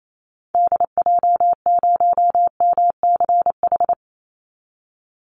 Morse code, heard as a keyed tone, about 28 wpm, 700 Hz. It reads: DJ0MCH
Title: Call Signs Repeated 2x - 28wpm